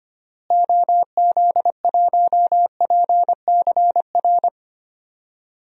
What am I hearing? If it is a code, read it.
Morse code, heard as a keyed tone, about 25 wpm, 700 Hz. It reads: OZ1PCR
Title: DXCC-100 Call Signs - 6 Characters - Repeated 2x - 25wpm